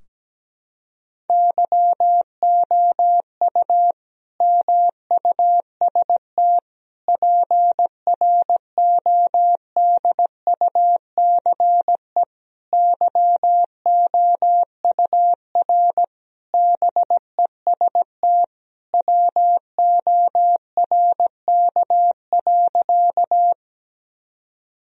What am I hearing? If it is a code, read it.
Morse code, heard as a keyed tone, about 17 wpm, 700 Hz. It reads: YOU MUST PRODUCE YOUR BEST WORK.
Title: Sentences from Top 1000 Words - Review Part 3 - 17wpm